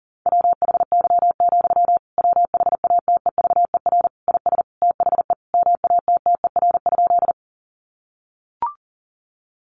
Morse code, heard as a keyed tone, about 40 wpm, 700 Hz. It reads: WHY, WHATEVER IS THE MATTER?
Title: Sentences from Top 1100 Words 40wpm